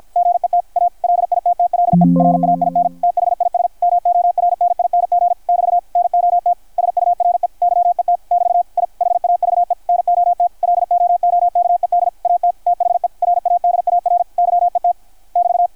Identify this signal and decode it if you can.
Morse code, heard as a keyed tone, about 53 wpm, 660 Hz. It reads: GET A LITTLE RAIN THIS MORNING <BT> NOT SURE YET <BT> I HAVE NOT LOOKED AT THE RADAR YET <BT>
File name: archived / CRSnet